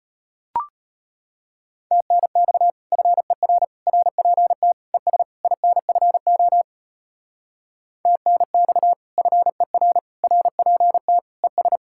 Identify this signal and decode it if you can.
Morse code, heard as a keyed tone, about 33 wpm, 700 Hz. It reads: TNX FER RPT ES INFO TNX FER RPT ES